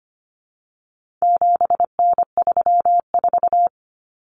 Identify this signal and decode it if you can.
Morse code, heard as a keyed tone, about 25 wpm, 700 Hz. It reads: 7N34
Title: Common Cut Numbers - 4-Digit 25wpm